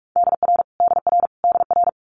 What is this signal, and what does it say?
Morse code, heard as a keyed tone, about 46 wpm, 700 Hz. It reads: DR DR DR
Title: Multi-letter Phonemes Repeated 3x 45wpm